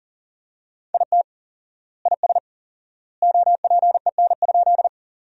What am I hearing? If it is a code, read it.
Morse code, heard as a keyed tone, about 40 wpm, 700 Hz. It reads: IT IS OPEN?
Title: 4x Word-Spacing - Sentences from Top 300 Words - 40wpm